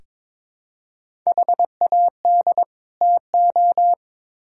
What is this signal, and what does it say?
Morse code, heard as a keyed tone, about 22 wpm, 700 Hz. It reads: HAD TO
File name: Build A Phrase - 2-Word Phrases - 22wpm